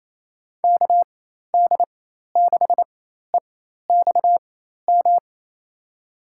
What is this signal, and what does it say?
Morse code, heard as a keyed tone, about 28 wpm, spaced as for 13 wpm, 700 Hz. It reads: KD6EXM
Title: USA Call Signs - 4x Character Spacing - 28wpm